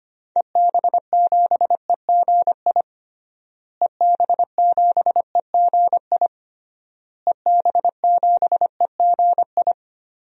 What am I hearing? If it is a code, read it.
Morse code, heard as a keyed tone, about 25 wpm, 700 Hz. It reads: EB7EGI EB7EGI EB7EGI
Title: DXCC-100 Call Signs - 6 Characters - Repeated 3x - 25wpm